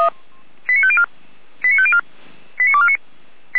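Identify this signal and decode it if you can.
Five-tone selective call (ZVEI), heard as quick runs of five tones, tones of about 70 ms. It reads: 89504 89504 89249
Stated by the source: ZVEI five-tone selective call received on SDR.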